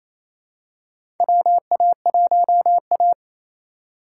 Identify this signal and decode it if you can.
Morse code, heard as a keyed tone, about 28 wpm, 700 Hz. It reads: WA1A
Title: USA Call Signs - 4 Characters - 4x Character Spacing - 28wpm